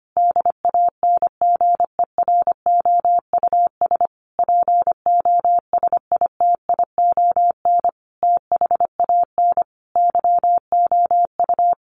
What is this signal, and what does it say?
Morse code, heard as a keyed tone, about 25 wpm, 700 Hz. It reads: DANGEROUS POSITION THAN YOU
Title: Sentences from Top 1400 Words 25wpm